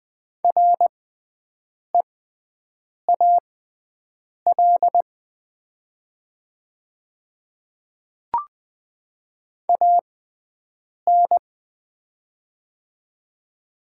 Morse code, heard as a keyed tone, about 21 wpm, spaced as for 7 wpm, 700 Hz. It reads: REAL AN
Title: 6x Character Spacing - Top 500 Words - 20wpm